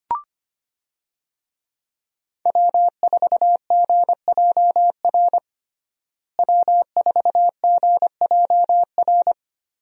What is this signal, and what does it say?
Morse code, heard as a keyed tone, about 25 wpm, 700 Hz. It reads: W4GJR W4GJR
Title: USA Call Signs - 5 Characters - Repeated 3x - 25wpm